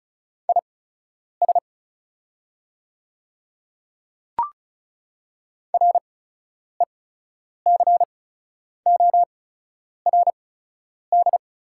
Morse code, heard as a keyed tone, about 36 wpm, spaced as for 10 wpm, 690 Hz. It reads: IS RECORD
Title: 8x Character Spacing - Top 500 Words - 35wpm